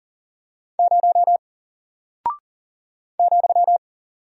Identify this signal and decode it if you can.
Morse code, heard as a keyed tone, about 40 wpm, 700 Hz. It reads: 0 ,
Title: Single Character (Letter:Number:Punctuation:Prosign) Alt 40wpm